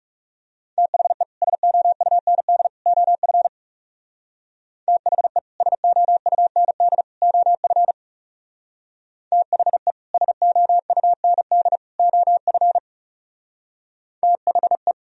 Morse code, heard as a keyed tone, about 37 wpm, 700 Hz. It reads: THE SOUND OF THE SOUND OF THE SOUND OF THE
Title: Speed-Racing - Sets of 3 Words 30wpm